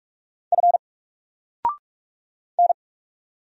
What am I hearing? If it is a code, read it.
Morse code, heard as a keyed tone, about 46 wpm, 700 Hz. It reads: F N
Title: Single Letter-Number 45wpm